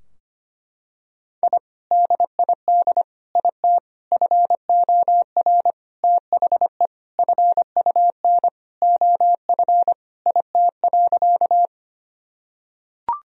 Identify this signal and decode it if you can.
Morse code, heard as a keyed tone, about 25 wpm, 700 Hz. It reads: I DID IT FOR THE FUN OF IT.
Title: Sentences from Top 1000 Words - Review Part 4 - 25wpm